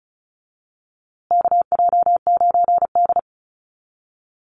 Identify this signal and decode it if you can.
Morse code, heard as a keyed tone, about 34 wpm, 700 Hz. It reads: KJ9D
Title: USA Call Signs Repeated 3x 35wpm